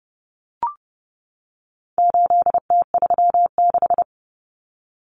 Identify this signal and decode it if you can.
Morse code, heard as a keyed tone, about 30 wpm, 700 Hz. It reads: 8T36
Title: Common Cut Numbers - 4-Digit 30wpm